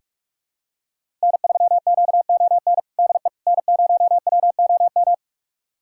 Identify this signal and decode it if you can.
Morse code, heard as a keyed tone, about 41 wpm, 700 Hz. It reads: N3QON DE N0WOM
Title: USA Callsign DE Callsign - 5 Characters - 45wpm